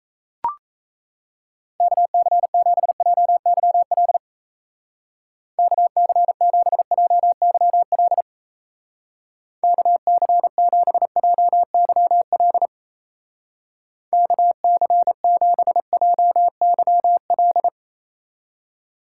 Morse code, heard as a keyed tone, about 33 wpm, 700 Hz. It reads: KC7JYL KC7JYL KC7JYL KC7JYL
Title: Speed Racing - USA Call Signs 28wpm